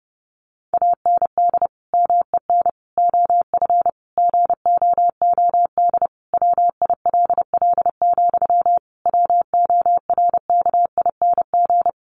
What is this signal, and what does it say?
Morse code, heard as a keyed tone, about 30 wpm, 700 Hz. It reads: AND MEN OF GOOD WILL, WORKING
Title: Sentences from Top 1400 Words 30wpm